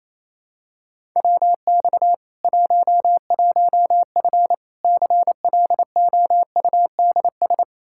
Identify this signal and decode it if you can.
Morse code, heard as a keyed tone, about 28 wpm, 700 Hz. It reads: WX 11F CLOUDS
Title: Practice QSO - Exchange 2 - Info - 28wpm